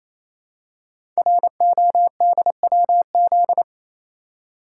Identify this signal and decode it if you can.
Morse code, heard as a keyed tone, about 28 wpm, 700 Hz. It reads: RODWZ
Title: Code Groups - 5 Letters - 28wpm